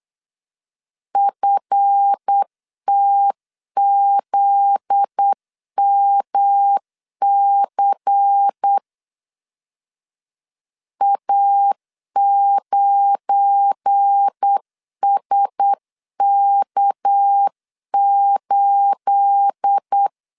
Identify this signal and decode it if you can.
Morse code, heard as a keyed tone, about 8 wpm, 800 Hz. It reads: FTZMC A9SK8
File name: LEZIONI / 9 28 8wpm 3512